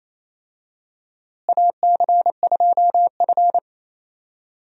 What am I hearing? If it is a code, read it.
Morse code, heard as a keyed tone, about 28 wpm, 700 Hz. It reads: AC2F
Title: USA Call Signs - 4 Characters - Repeated 3x - 28wpm